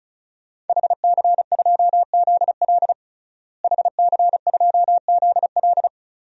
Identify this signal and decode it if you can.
Morse code, heard as a keyed tone, about 34 wpm, 700 Hz. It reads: HC2ZL HC2ZL
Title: DXCC-100 Call Signs - 5 Characters - Repeated 2x - 35wpm